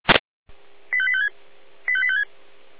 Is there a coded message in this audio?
Yes: five-tone selective call (ZVEI) from 0.9 to 2.2 s, heard as quick runs of five tones, tones of about 70 ms. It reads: 85856 85856